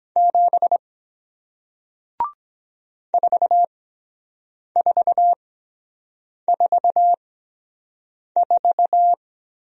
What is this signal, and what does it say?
Morse code, heard as a keyed tone, about 23 wpm, spaced as for 7 wpm, 700 Hz. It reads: 7 4444